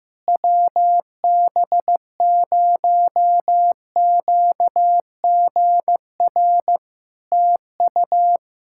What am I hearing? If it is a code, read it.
Morse code, heard as a keyed tone, about 15 wpm, 700 Hz. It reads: WB0QGR TU
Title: Practice QSO - Exchange 4 - Ending - 15wpm